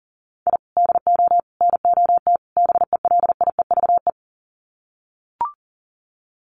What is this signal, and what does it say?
Morse code, heard as a keyed tone, about 40 wpm, 700 Hz. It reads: I DO NOT BELIEVE
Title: Sets of 4 Words 40wpm